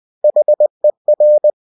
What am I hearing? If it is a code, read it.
Morse code, heard as a keyed tone, about 20 wpm, 600 Hz. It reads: HER